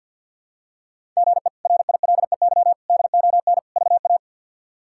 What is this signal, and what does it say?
Morse code, heard as a keyed tone, about 52 wpm, 700 Hz. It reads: GE RILEY DON VA
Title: K1USN SST QSO Practice - 50wpm